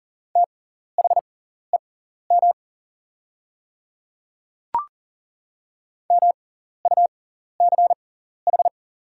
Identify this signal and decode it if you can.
Morse code, heard as a keyed tone, about 40 wpm, spaced as for 14 wpm, 700 Hz. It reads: THEM MUCH
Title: 6x Character Spacing - Top 200 Words - 40wpm